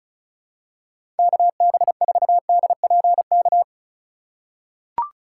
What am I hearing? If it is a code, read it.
Morse code, heard as a keyed tone, about 36 wpm, 700 Hz. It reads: KB4DPK
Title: USA Call Signs - 4x Character Spacing - 35wpm